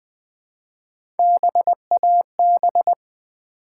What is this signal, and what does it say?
Morse code, heard as a keyed tone, about 20 wpm, 700 Hz. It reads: BAB
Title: Build A Word - 1 Letter At A Time - 4-Letter Words - 20wpm